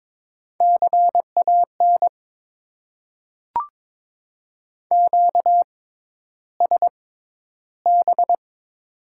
Morse code, heard as a keyed tone, about 22 wpm, spaced as for 11 wpm, 700 Hz. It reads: CAN QSB
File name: Top 90 QSO Elements - 6x Character Spacing - 22wpm